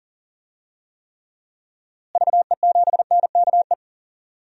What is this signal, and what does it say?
Morse code, heard as a keyed tone, about 40 wpm, 700 Hz. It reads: VE7NKE
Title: DXCC-100 Call Signs - 6 Characters - Repeated 3x - 40wpm